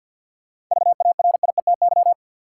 Morse code, heard as a keyed tone, about 46 wpm, 700 Hz. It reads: VARIETY
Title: Sentences from Top 1000 Words - Review Part 4 - New Word Drill - 50wpm